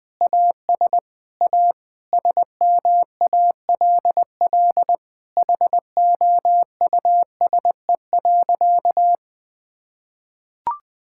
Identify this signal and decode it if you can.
Morse code, heard as a keyed tone, about 21 wpm, 700 Hz. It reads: AS A SMALL HOUSE.
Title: Sentences with 8-Word Phrases - Spoken Beginning - 20wpm